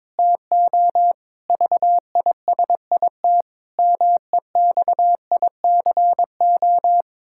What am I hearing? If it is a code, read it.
Morse code, heard as a keyed tone, about 22 wpm, 700 Hz. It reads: TO VISIT MEXICO
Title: Pangram Challenge v2 - 22wpm